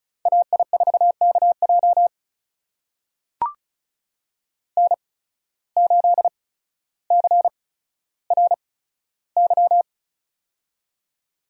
Morse code, heard as a keyed tone, about 36 wpm, spaced as for 15 wpm, 700 Hz. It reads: AI4KJ N8CRY
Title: USA Call Signs - 5 Characters - 8x Character Spacing - 35wpm